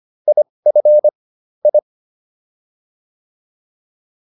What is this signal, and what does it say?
Morse code, heard as a keyed tone, about 25 wpm, spaced as for 20 wpm, 600 Hz. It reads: IF I